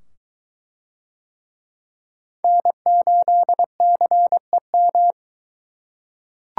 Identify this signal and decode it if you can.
Morse code, heard as a keyed tone, about 21 wpm, 700 Hz. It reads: N8CEM